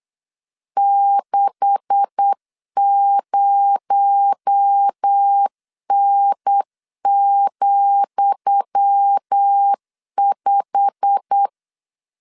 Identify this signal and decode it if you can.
Morse code, heard as a keyed tone, about 8 wpm, 800 Hz. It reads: 60N,5
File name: LEZIONI / 10 SIGNS / 10D COMA 8wpm 3749